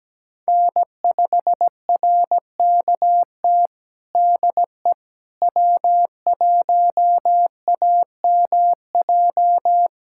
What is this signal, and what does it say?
Morse code, heard as a keyed tone, about 17 wpm, 700 Hz. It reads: N5RKT DE W1AMJ
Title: USA Callsign DE Callsign - 5 Characters - 17wpm